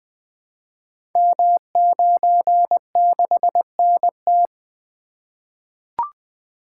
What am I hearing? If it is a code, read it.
Morse code, heard as a keyed tone, about 20 wpm, 700 Hz. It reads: M96NT
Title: Code Groups - 5 Letter:Numbers - 20wpm